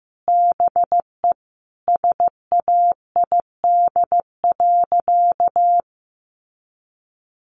Morse code, heard as a keyed tone, about 15 wpm, 700 Hz. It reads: BE SAID.
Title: Sentences from Top 100 Words 15wpm